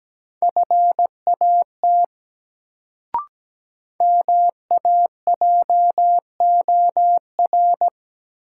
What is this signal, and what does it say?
Morse code, heard as a keyed tone, about 17 wpm, 700 Hz. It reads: FAT MAJOR
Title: Sentences from Top 600 Words - New Word Drill - 17wpm